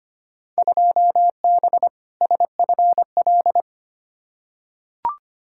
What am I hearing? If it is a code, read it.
Morse code, heard as a keyed tone, about 25 wpm, 700 Hz. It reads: 2B SFL
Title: Field Day Exchanges Encore - 25wpm